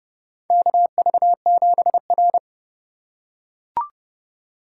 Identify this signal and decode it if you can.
Morse code, heard as a keyed tone, about 30 wpm, 700 Hz. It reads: KV7R
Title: USA Call Signs - 4 Characters - 4x Character Spacing - 30wpm